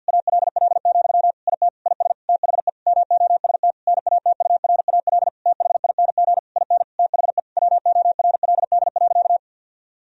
Morse code, heard as a keyed tone, about 50 wpm, 700 Hz. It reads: ALL, IT IS THE MOST NATURAL THING IN THE WORLD.
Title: Sentences from Top 1000 Words - Review Part 4 - 50wpm